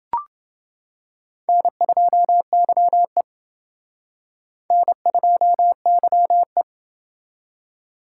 Morse code, heard as a keyed tone, about 29 wpm, 700 Hz. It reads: N2YE N2YE